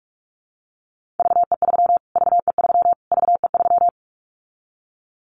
Morse code, heard as a keyed tone, about 44 wpm, 700 Hz. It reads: VE3 VE3 VE3
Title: Callsign Prefixes for Top 25 DXCC - Repeated 3x - 45wpm